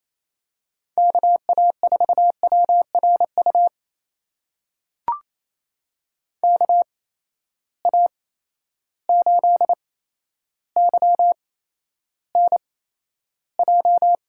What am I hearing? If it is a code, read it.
Morse code, heard as a keyed tone, about 28 wpm, spaced as for 12 wpm, 700 Hz. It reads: KA4WRU KA8YNJ
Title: USA Call Signs - 6 Characters - 8x Character Spacing - 28wpm